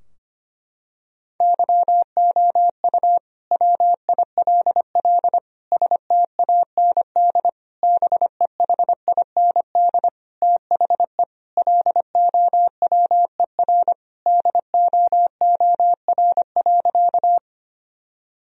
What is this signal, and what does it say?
Morse code, heard as a keyed tone, about 25 wpm, 700 Hz. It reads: YOU WILL STAND BEHIND THE LOWER DOOR.
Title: Sentences from Top 1000 Words - Review Part 4 - 25wpm